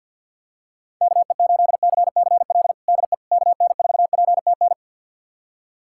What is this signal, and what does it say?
Morse code, heard as a keyed tone, about 48 wpm, 700 Hz. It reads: KE8CCL DE KN4PTN
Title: USA Callsign DE Callsign - 6 Characters - 50wpm